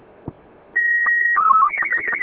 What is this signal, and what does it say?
SSTV (slow-scan television), Robot 36.